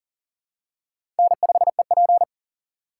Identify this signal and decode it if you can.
Morse code, heard as a keyed tone, about 40 wpm, 700 Hz. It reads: N5EP
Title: USA Call Signs - 2x Character Spacing - 40wpm